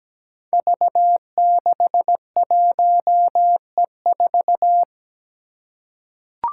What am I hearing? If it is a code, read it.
Morse code, heard as a keyed tone, about 17 wpm, 700 Hz. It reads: V61E4
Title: Five Challenging Letter-Number Combinations 17wpm